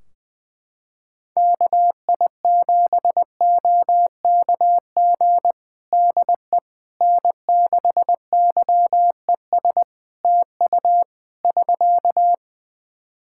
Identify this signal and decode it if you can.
Morse code, heard as a keyed tone, about 20 wpm, 700 Hz. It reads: KI7OKG DE N6YES TU <SK>
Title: Practice QSO - Exchange 4 - Ending - 20wpm